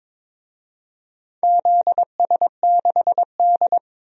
Morse code, heard as a keyed tone, about 22 wpm, 700 Hz. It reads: ZS6D
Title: DXCC-100 Call Signs - 4 Characters - 22wpm